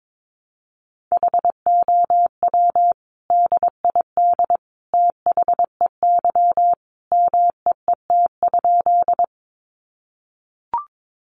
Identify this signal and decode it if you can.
Morse code, heard as a keyed tone, about 22 wpm, 700 Hz. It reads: HOW DID THEY MEET?
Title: Sentences from Top 1100 Words 22wpm